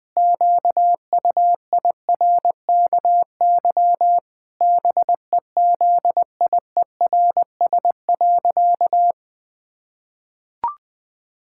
Morse code, heard as a keyed tone, about 21 wpm, 700 Hz. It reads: QUIRKY BEZIERS.
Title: Pangram Challenge v2 - 20wpm